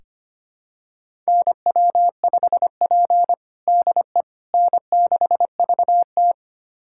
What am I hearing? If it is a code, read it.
Morse code, heard as a keyed tone, about 25 wpm, 700 Hz. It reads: NW5P DE N6VT
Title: USA Callsign DE Callsign - 4 Characters - 25wpm